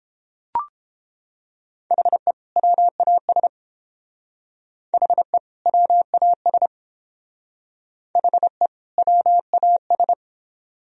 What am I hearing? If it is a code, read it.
Morse code, heard as a keyed tone, about 27 wpm, 700 Hz. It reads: HE WAS HE WAS HE WAS
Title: Speed-Racing - Sets of 2 Words 22wpm